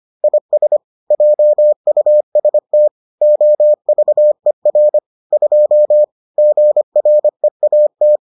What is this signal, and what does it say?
Morse code, heard as a keyed tone, about 25 wpm, 600 Hz. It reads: IS JUST OVER 2 GREAT